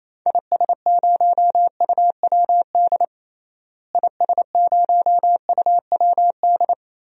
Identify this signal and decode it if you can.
Morse code, heard as a keyed tone, about 28 wpm, 700 Hz. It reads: IS0UWD IS0UWD
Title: DXCC-100 Call Signs - 6 Characters - Repeated 2x - 28wpm